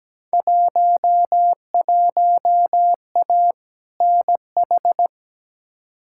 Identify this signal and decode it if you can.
Morse code, heard as a keyed tone, about 17 wpm, 700 Hz. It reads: 11A NH
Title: Field Day Exchanges Encore - 17wpm